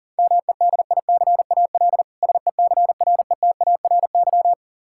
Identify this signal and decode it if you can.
Morse code, heard as a keyed tone, about 40 wpm, 700 Hz. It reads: MEDICAL SECRETARY
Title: Practice QSO - Exchange 3 - More Info - 40wpm